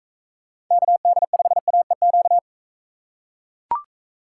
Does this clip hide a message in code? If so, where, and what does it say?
Yes: Morse code from 0.7 to 2.4 s, heard as a keyed tone, about 42 wpm, 700 Hz. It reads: KD5AEQ